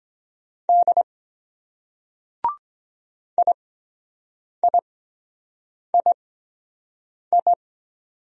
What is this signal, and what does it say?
Morse code, heard as a keyed tone, about 23 wpm, spaced as for 7 wpm, 700 Hz. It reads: D IIII